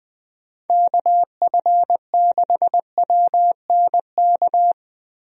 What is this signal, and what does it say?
Morse code, heard as a keyed tone, about 21 wpm, 700 Hz. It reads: KF6WNK
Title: USA Call Signs - 6 Characters - Repeated 3x - 20wpm